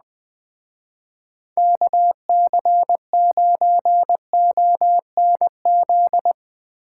Morse code, heard as a keyed tone, about 19 wpm, 700 Hz. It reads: KC9ONZ
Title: Speed Racing - USA 2x3 Call Signs 17wpm